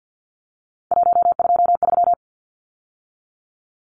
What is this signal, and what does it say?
Morse code, heard as a keyed tone, about 50 wpm, 700 Hz. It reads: 123
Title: Numbers - 3-Digit 50wpm